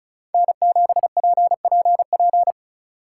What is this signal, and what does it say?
Morse code, heard as a keyed tone, about 42 wpm, 700 Hz. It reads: N7PPP
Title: Speed Racing - USA 1x3 Call Signs 35wpm